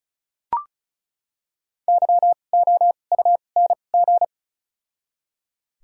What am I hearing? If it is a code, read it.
Morse code, heard as a keyed tone, about 36 wpm, 700 Hz. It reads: Y O U N G
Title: 2x Character Spacing - Top 500 Words - 35wpm